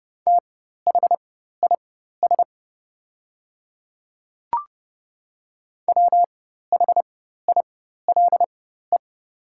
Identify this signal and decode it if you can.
Morse code, heard as a keyed tone, about 30 wpm, spaced as for 14 wpm, 700 Hz. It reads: THIS WHILE
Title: 4x Character Spacing - Top 300 Words - 30wpm